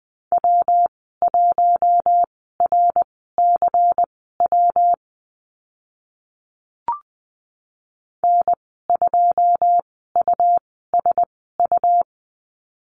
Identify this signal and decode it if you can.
Morse code, heard as a keyed tone, about 20 wpm, spaced as for 14 wpm, 700 Hz. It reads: W1RCW N2USU
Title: 128 Lesson 21g - Number 6 - Call Signs 20wpm